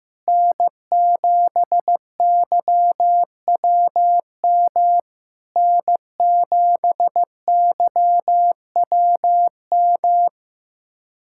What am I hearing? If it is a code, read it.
Morse code, heard as a keyed tone, about 15 wpm, 700 Hz. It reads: N7YWM N7YWM
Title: SOTA QSO Practice 15wpm